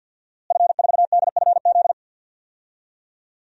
Morse code, heard as a keyed tone, about 52 wpm, 700 Hz. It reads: F4DFZ
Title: DXCC-100 Call Signs - 5 Characters - Repeated 2x - 50wpm